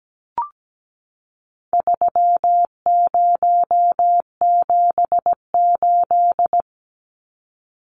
Morse code, heard as a keyed tone, about 17 wpm, 700 Hz. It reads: 3078
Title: Numbers - 4-Digit 17wpm